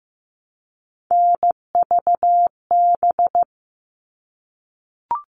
Morse code, heard as a keyed tone, about 15 wpm, 700 Hz. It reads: NVB
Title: All Cut Numbers - 3-Digit 15wpm